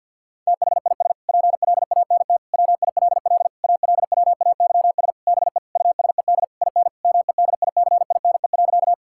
Morse code, heard as a keyed tone, about 50 wpm, 700 Hz. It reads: THIS PLANT WILL ALWAYS BE USED IN MEDICINE.
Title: Sentences from Top 1000 Words - Review Part 6 - 50wpm